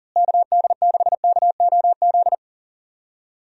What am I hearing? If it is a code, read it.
Morse code, heard as a keyed tone, about 40 wpm, 700 Hz. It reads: KD6KOZ
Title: USA Call Signs - 6 Characters - Repeated 2x - 40wpm